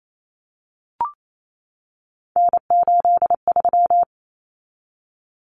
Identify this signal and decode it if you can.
Morse code, heard as a keyed tone, about 28 wpm, 700 Hz. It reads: N83
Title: Common Cut Numbers - 3-Digit 28wpm